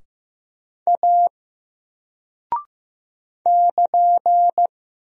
Morse code, heard as a keyed tone, about 15 wpm, 700 Hz. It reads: A <KN>
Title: Single Character (Letter:Number:Punctuation:Prosign) 15wpm